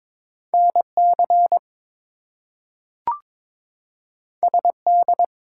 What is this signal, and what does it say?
Morse code, heard as a keyed tone, about 22 wpm, 700 Hz. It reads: NC SD
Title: US States and Canadian Provinces 22wpm